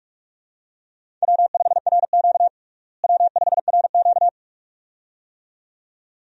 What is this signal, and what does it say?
Morse code, heard as a keyed tone, about 48 wpm, 700 Hz. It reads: W5RQ W5RQ
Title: USA Call Signs - 4 Characters - Repeated 2x - 45wpm